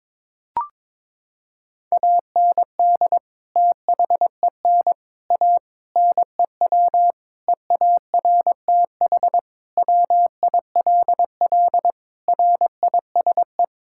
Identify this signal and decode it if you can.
Morse code, heard as a keyed tone, about 22 wpm, 700 Hz. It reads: AND THEN A NEW EARTH WILL RISE
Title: Sentences from Top 1000 Words - Review Part 6 - 22wpm